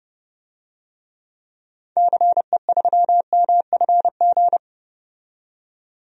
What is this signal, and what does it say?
Morse code, heard as a keyed tone, about 30 wpm, 700 Hz. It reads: CE3MFG
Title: DXCC-100 Call Signs - 6 Characters - 30wpm